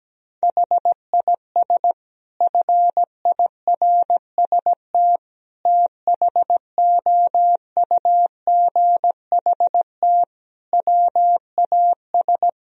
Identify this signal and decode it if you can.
Morse code, heard as a keyed tone, about 17 wpm, 700 Hz. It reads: HIS FIRST THOUGHT WAS
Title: Sentences from Top 1000 Words - Review Part 3 - 17wpm